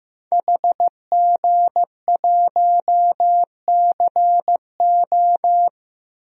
Morse code, heard as a keyed tone, about 15 wpm, 700 Hz. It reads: HG1CO
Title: DXCC-100 Call Signs - 5 Characters - 15wpm